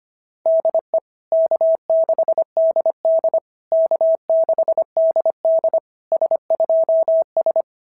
Morse code, heard as a keyed tone, about 24 wpm, 650 Hz. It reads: DE K6DD K6DD S2S
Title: SOTA QSO Practice 25wpm